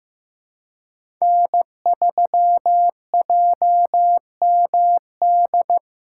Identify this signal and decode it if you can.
Morse code, heard as a keyed tone, about 15 wpm, 700 Hz. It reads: N3JMD
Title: USA Call Signs - 5 Characters - 2x Character Spacing - 15wpm